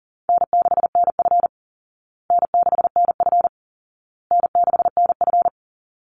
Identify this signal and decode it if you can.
Morse code, heard as a keyed tone, about 40 wpm, 700 Hz. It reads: N6NF N6NF N6NF
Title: Call Signs Repeated 2x - 40wpm